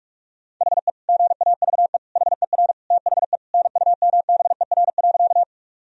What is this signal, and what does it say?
Morse code, heard as a keyed tone, about 46 wpm, 700 Hz. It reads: HE GAVE HER THE NUMBER.
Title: Sentences from Top 1000 Words - Review Part 1 - 45wpm